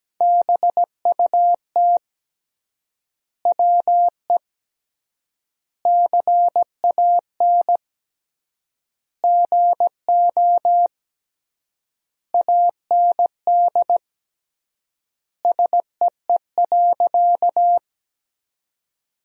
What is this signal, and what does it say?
Morse code, heard as a keyed tone, about 17 wpm, 700 Hz. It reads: BUT WE CAN GO AND SEE.
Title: 3x Word-Spacing - Sentences from Top 100 Words - 17wpm